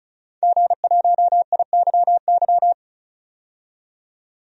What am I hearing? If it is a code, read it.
Morse code, heard as a keyed tone, about 36 wpm, 700 Hz. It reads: G1IYY
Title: DXCC-100 Call Signs - 5 Characters - 35wpm